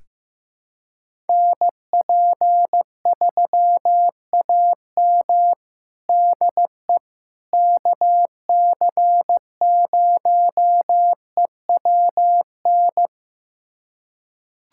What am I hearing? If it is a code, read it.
Morse code, heard as a keyed tone, about 15 wpm, 700 Hz. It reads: NP3AM DE KC0EWN
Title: USA Callsign DE Callsign 15wpm